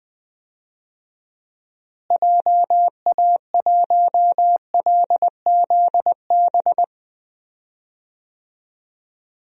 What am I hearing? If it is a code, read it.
Morse code, heard as a keyed tone, about 21 wpm, 690 Hz. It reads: JA1LZB
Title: DXCC-100 Call Signs - 6 Characters - Repeated 2x - 20wpm